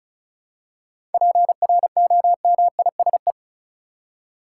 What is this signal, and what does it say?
Morse code, heard as a keyed tone, about 34 wpm, 700 Hz. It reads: PROMISE
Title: Seven Letter Words - 35wpm